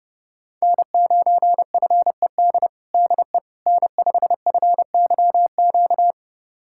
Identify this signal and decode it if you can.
Morse code, heard as a keyed tone, about 29 wpm, 700 Hz. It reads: N9FED DE N5FYQ
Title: USA Callsign DE Callsign - 5 Characters - 30wpm